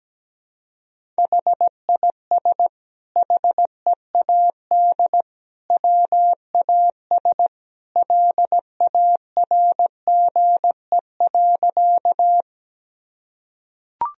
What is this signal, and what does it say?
Morse code, heard as a keyed tone, about 17 wpm, 700 Hz. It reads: HIS HEAD WAS LARGE.
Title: Sentences from Top 1000 Words - Review Part 1 - 17wpm